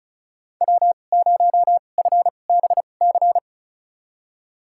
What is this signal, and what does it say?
Morse code, heard as a keyed tone, about 34 wpm, spaced as for 25 wpm, 700 Hz. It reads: W0FBC
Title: USA Call Signs - 2x Character Spacing - 35wpm